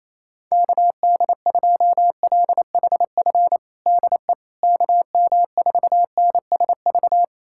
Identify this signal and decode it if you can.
Morse code, heard as a keyed tone, about 28 wpm, 700 Hz. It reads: KD2LHF DE KM4NSV
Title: USA Callsign DE Callsign - 6 Characters - 28wpm